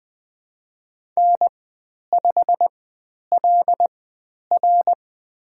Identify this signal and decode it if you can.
Morse code, heard as a keyed tone, about 20 wpm, spaced as for 10 wpm, 700 Hz. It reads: N5LR
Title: 109 Lesson 18g - Letter m - Call Signs 20wpm